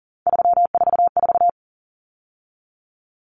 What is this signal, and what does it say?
Morse code, heard as a keyed tone, about 40 wpm, 700 Hz. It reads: 344
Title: Numbers - 3-Digit 40wpm